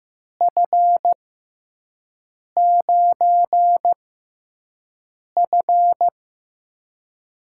Morse code, heard as a keyed tone, about 15 wpm, spaced as for 5 wpm, 700 Hz. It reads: F9F
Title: USA Call Signs - 5 Characters - 6x Character Spacing - 15wpm